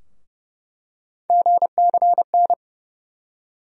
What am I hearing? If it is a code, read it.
Morse code, heard as a keyed tone, about 30 wpm, 700 Hz. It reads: GCN
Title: USA Call Sign Postfixes - 3 Letters 30wpm